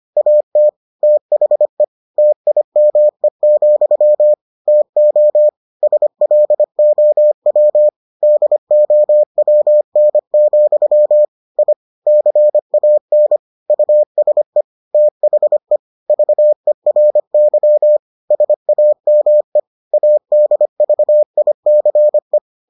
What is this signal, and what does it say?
Morse code, heard as a keyed tone, about 25 wpm, 600 Hz. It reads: AT THE TIME, TO SLOW DOWN, I CAN USE THE VERY SAME ADVICE